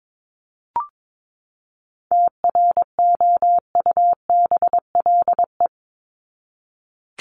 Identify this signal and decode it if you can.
Morse code, heard as a keyed tone, about 22 wpm, 700 Hz. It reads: TROUBLE
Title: Top 1001-1250 Words 22wpm